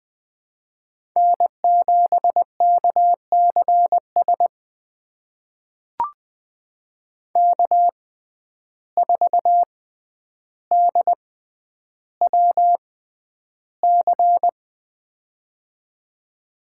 Morse code, heard as a keyed tone, about 20 wpm, spaced as for 10 wpm, 700 Hz. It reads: N7KCS K4DWC
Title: USA Call Signs - 5 Characters - 6x Character Spacing - 20wpm